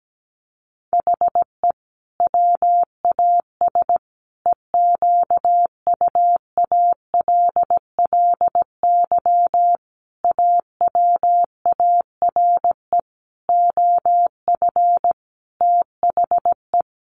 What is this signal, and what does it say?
Morse code, heard as a keyed tone, about 17 wpm, 700 Hz. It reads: HE WAS EQUALLY AWARE OF THE
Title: Sentences from Top 1400 Words 17wpm